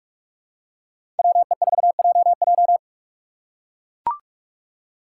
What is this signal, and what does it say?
Morse code, heard as a keyed tone, about 44 wpm, 700 Hz. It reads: WE4JJ